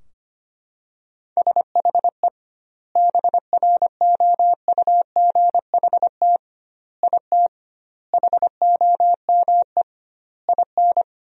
Morse code, heard as a keyed tone, about 25 wpm, 700 Hz. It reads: SHE BROUGHT IT HOME IN
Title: 2x Word-Spacing - Sentences from Top 500 Words - 25wpm